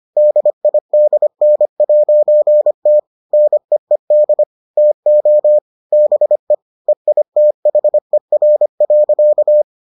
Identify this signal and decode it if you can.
Morse code, heard as a keyed tone, about 25 wpm, 600 Hz. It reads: DIDN'T NEED TO BE EITHER.